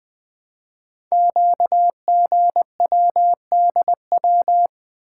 Morse code, heard as a keyed tone, about 20 wpm, 700 Hz. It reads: QGWDW
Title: Code Groups - 5 Letters - 20wpm